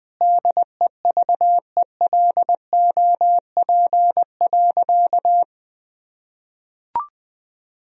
Morse code, heard as a keyed tone, about 19 wpm, 700 Hz. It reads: DEVELOP.
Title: Sentences from Top 1000 Words - Review Part 2 - 20wpm